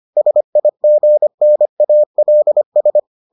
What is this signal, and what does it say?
Morse code, heard as a keyed tone, about 25 wpm, 600 Hz. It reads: SIGNALS